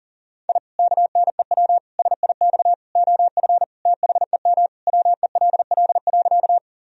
Morse code, heard as a keyed tone, about 40 wpm, 700 Hz. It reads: I KNEW SIX OF THEM WELL.
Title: Sentences from Top 1000 Words - Review Part 2 - 40wpm